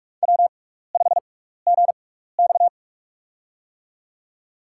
Morse code, heard as a keyed tone, about 46 wpm, spaced as for 16 wpm, 700 Hz. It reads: W5GX
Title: USA Call Signs - 4 Characters - 6x Character Spacing - 45wpm